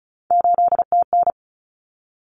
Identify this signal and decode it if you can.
Morse code, heard as a keyed tone, about 34 wpm, 700 Hz. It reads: 8TN
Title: Common Cut Numbers - 3-Digit 35wpm